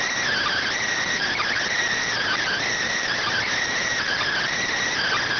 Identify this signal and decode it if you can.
Emergency Alert System (SAME) tones (header bursts), heard as harsh buzzing bursts. It is heard from 0.3 to 4.5 s.
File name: findings / CT SAGITAL 27.gen-18 rx